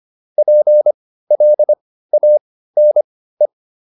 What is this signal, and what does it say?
Morse code, heard as a keyed tone, about 25 wpm, spaced as for 15 wpm, 600 Hz. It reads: PLANE
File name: SS101 15